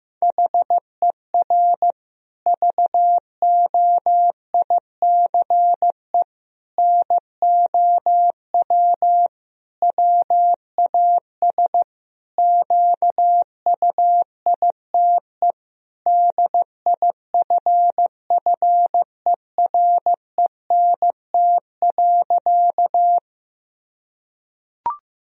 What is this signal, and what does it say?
Morse code, heard as a keyed tone, about 15 wpm, 700 Hz. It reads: HER VOICE NOW WAS QUITE DIFFERENT.
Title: Sentences from Top 1000 Words - Review Part 3 - 15wpm